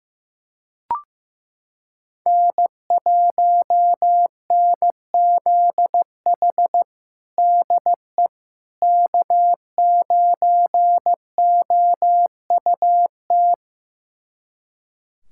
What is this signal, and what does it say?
Morse code, heard as a keyed tone, about 16 wpm, 700 Hz. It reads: N1NZH DE K9OUT
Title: USA Callsign DE Callsign - 5 Characters - 15wpm